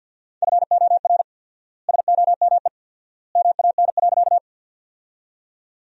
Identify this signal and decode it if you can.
Morse code, heard as a keyed tone, about 48 wpm, 700 Hz. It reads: FOR SOME MAN.
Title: 4x Word-Spacing - Sentences from Top 200 Words - 50wpm